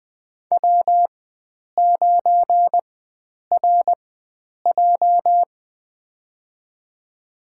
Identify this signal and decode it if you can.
Morse code, heard as a keyed tone, about 20 wpm, spaced as for 9 wpm, 700 Hz. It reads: W9RJ
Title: USA Call Signs - 4 Characters - 4x Character Spacing - 20wpm